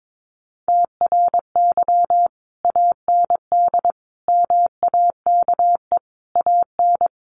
Morse code, heard as a keyed tone, about 22 wpm, 700 Hz. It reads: TRY AND MAKE AN
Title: Sentences from Top 1200 Words 22wpm